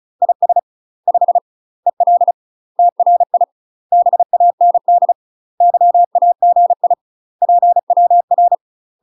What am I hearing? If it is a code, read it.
Morse code, heard as a keyed tone, about 36 wpm, 700 Hz. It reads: IS 5 EL TRI BAND YAGI PWR
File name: UK MASTER QSOs TRAINING / 35WPM / UK MASTER QSOs 0007